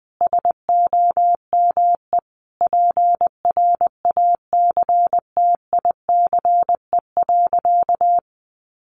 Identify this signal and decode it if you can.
Morse code, heard as a keyed tone, about 20 wpm, 700 Hz. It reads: SOME PRACTICE.
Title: Sentences from Top 900 Words 20wpm